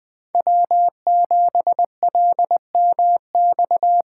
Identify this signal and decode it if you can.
Morse code, heard as a keyed tone, about 24 wpm, 700 Hz. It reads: W7LMX
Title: Speed Racing - USA 1x3 Call Signs 20wpm